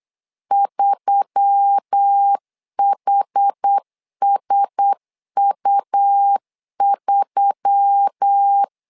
Morse code, heard as a keyed tone, about 8 wpm, 800 Hz. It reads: 3HSU3
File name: LEZIONI / 4 Q1S3 8wpm 3037